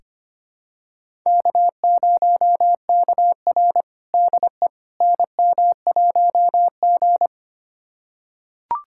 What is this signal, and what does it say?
Morse code, heard as a keyed tone, about 25 wpm, 700 Hz. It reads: K0KR DE NM1G
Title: USA Callsign DE Callsign - 4 Characters - 25wpm